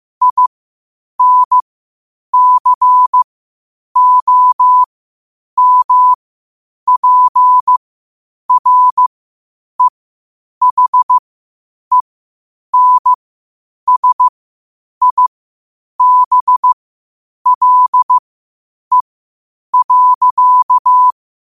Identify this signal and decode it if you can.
Morse code, heard as a keyed tone, about 15 wpm, spaced as for 8 wpm, 1000 Hz. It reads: INCOMPREHENSIBLE.
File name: qotdfiles / 8wpm